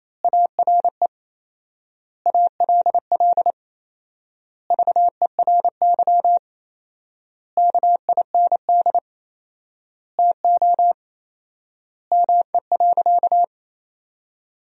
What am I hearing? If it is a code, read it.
Morse code, heard as a keyed tone, about 28 wpm, 700 Hz. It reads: ARE ALL VERY KIND TO ME.
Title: 4x Word-Spacing - Sentences from Top 200 Words - 28wpm